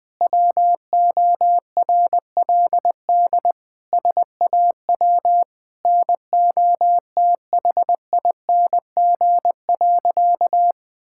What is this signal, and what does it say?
Morse code, heard as a keyed tone, about 20 wpm, 700 Hz. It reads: WORLD SAW NOTHING.
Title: Sentences from Top 1000 Words - Review Part 6 - 20wpm